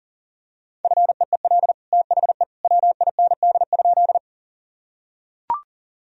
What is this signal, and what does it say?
Morse code, heard as a keyed tone, about 40 wpm, 700 Hz. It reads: FEEL THE WIND?
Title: Sentences from Top 1000 Words - Review Part 2 - 40wpm